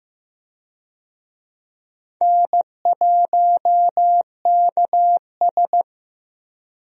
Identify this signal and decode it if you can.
Morse code, heard as a keyed tone, about 15 wpm, 700 Hz. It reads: N1KS
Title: USA Call Signs - 4 Characters - 15wpm